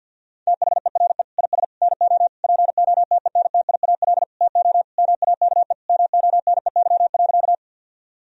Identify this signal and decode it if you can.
Morse code, heard as a keyed tone, about 50 wpm, 700 Hz. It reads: THERE IS NO POTENTIAL TO MAKE MONEY.
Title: Sentences from Top 1000 Words - Review Part 4 - 50wpm